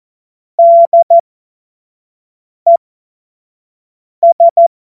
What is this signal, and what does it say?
Morse code, heard as a keyed tone, about 14 wpm, spaced as for 5 wpm, 680 Hz. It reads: DES
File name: Morsekurs / Witze / Witz 31. Januar 2023 25 F70